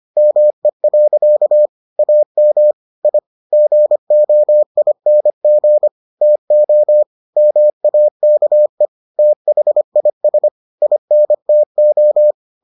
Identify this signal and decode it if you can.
Morse code, heard as a keyed tone, about 25 wpm, 600 Hz. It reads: ME. AM I GOING TO MAKE THIS INTO